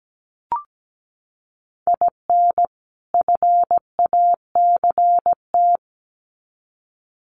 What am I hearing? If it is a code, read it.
Morse code, heard as a keyed tone, about 17 wpm, 700 Hz. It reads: IN FACT
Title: Sets of 2 Words 17wpm